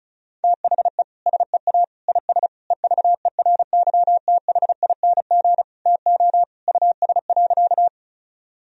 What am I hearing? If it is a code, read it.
Morse code, heard as a keyed tone, about 34 wpm, 700 Hz. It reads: THE SEA IS EVERYTHING TO US.
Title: Sentences from Top 1000 Words - Review Part 1 - 35wpm